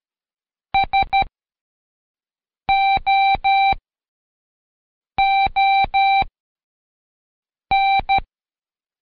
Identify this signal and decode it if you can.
Morse code, heard as a keyed tone, about 13 wpm, spaced as for 5 wpm, 780 Hz. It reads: SOON
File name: Code Quick / AudioDownld2 / Seg102-Day12-FinalDrill